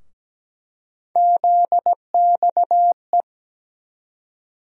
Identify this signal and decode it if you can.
Morse code, heard as a keyed tone, about 17 wpm, 700 Hz. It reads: ZXE
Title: USA Call Sign Postfixes - 3 Letters 17wpm